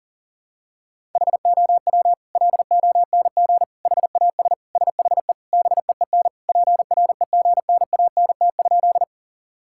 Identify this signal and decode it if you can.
Morse code, heard as a keyed tone, about 40 wpm, 700 Hz. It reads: HOW LONG HAS SHE BEEN PREGNANT?
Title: Sentences from Top 1000 Words - Review Part 6 - 40wpm